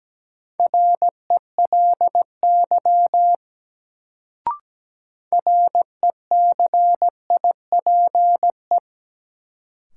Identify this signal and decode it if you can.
Morse code, heard as a keyed tone, about 17 wpm, 700 Hz. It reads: RELY RECIPE
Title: Sentences from Top 1200 Words - New Word Drill Alt - 17wpm